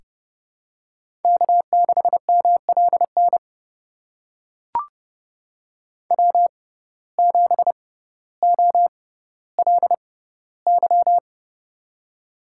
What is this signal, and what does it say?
Morse code, heard as a keyed tone, about 29 wpm, spaced as for 15 wpm, 700 Hz. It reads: K6MLN W7OLY
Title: USA Call Signs - 5 Characters - 6x Character Spacing - 30wpm